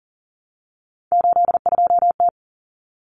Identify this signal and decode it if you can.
Morse code, heard as a keyed tone, about 40 wpm, 700 Hz. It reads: 82T
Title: Common Cut Numbers - 3-Digit 40wpm